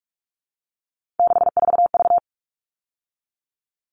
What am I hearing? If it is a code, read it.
Morse code, heard as a keyed tone, about 44 wpm, 700 Hz. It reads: 64V
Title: All Cut Numbers - 3-Digit 45wpm